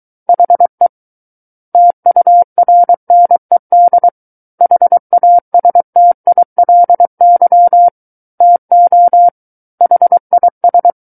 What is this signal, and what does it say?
Morse code, heard as a keyed tone, about 23 wpm, 700 Hz. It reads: HE TURNED HASTILY TO HIS
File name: Book One / 14-Chapter 14 - 23 WPM